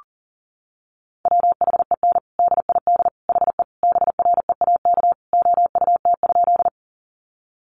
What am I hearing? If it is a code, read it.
Morse code, heard as a keyed tone, about 40 wpm, 700 Hz. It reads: WHEN DID HE BREAK OUT?
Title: Sentences from Top 900 Words 40wpm